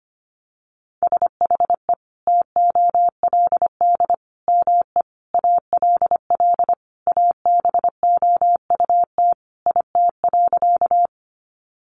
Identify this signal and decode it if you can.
Morse code, heard as a keyed tone, about 25 wpm, 700 Hz. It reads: SHE TOLD ME ALL ABOUT IT.
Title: Sentences from Top 400 Words 25wpm